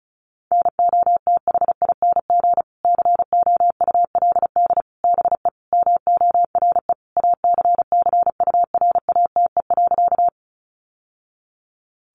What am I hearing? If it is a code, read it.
Morse code, heard as a keyed tone, about 34 wpm, 700 Hz. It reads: NOTHING COULD BE MORE ACCURATE.
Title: Sentences from Top 1200 Words 35wpm